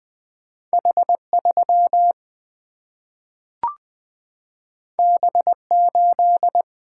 Morse code, heard as a keyed tone, about 20 wpm, 700 Hz. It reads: H3 B8
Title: Single-Letter Single-Number 20wpm